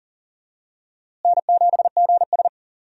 Morse code, heard as a keyed tone, about 40 wpm, 700 Hz. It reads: N7GS
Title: USA Call Signs - 4 Characters - Repeated 2x - 40wpm